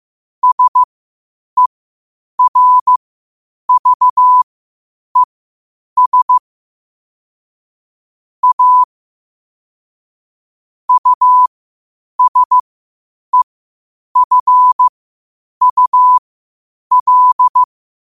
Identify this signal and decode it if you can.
Morse code, heard as a keyed tone, about 15 wpm, spaced as for 8 wpm, 1000 Hz. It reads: SERVES A USEFUL
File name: qotdfiles / 8wpm